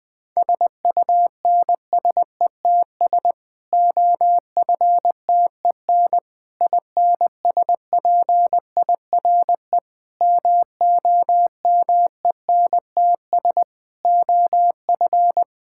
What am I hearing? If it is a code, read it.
Morse code, heard as a keyed tone, about 20 wpm, 700 Hz. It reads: SUNSETS OFTEN INSPIRE MOMENTS OF
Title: Difficult Sentences with Hints - 20wpm